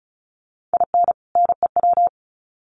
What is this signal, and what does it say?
Morse code, heard as a keyed tone, about 34 wpm, 700 Hz. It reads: IN NEW
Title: Sets of 2 Words 35wpm